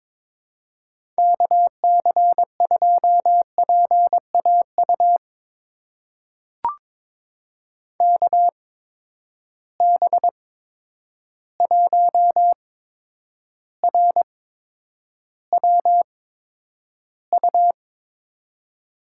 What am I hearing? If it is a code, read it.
Morse code, heard as a keyed tone, about 22 wpm, spaced as for 9 wpm, 700 Hz. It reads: KC2PAU KB1RWU
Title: USA Call Signs - 6 Characters - 8x Character Spacing - 22wpm